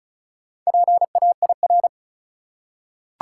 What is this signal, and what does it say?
Morse code, heard as a keyed tone, about 36 wpm, 700 Hz. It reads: PAIR
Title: Four Letter Words - 35wpm